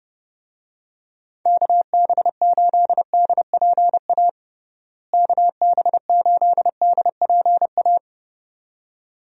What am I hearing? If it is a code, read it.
Morse code, heard as a keyed tone, about 31 wpm, 700 Hz. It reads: KB8DPA KB8DPA
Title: USA Call Signs - 6 Characters - Repeated 2x - 30wpm